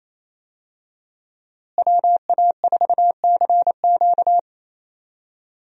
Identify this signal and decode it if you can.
Morse code, heard as a keyed tone, about 28 wpm, 700 Hz. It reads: WA4CQ
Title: USA Call Signs - 5 Characters - Repeated 2x - 28wpm